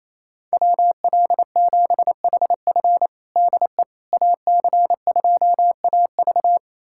Morse code, heard as a keyed tone, about 28 wpm, 700 Hz. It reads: WL7HF DE AC2AV
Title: USA Callsign DE Callsign - 5 Characters - 28wpm